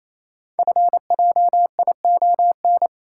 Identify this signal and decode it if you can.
Morse code, heard as a keyed tone, about 27 wpm, 700 Hz. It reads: FJION
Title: Code Groups - 5 Letters - 28wpm